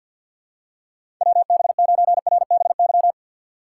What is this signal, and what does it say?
Morse code, heard as a keyed tone, about 53 wpm, 700 Hz. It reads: WB9RBY
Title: USA Call Signs - 6 Characters - 6x Character Spacing - 50wpm